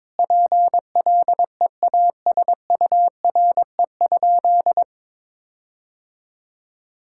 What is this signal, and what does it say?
Morse code, heard as a keyed tone, about 22 wpm, 700 Hz. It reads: PLEASURE?
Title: Sentences from Top 1600 Words 22wpm